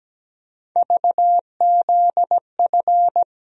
Morse code, heard as a keyed tone, about 17 wpm, 700 Hz. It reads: VZF
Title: USA Call Sign Postfixes - 3 Letters 17wpm